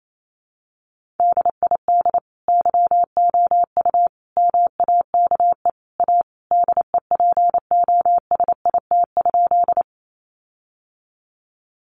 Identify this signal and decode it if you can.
Morse code, heard as a keyed tone, about 28 wpm, 700 Hz. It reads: DID YOU MAKE A DEPOSIT?
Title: Sentences from Top 1300 Words 28wpm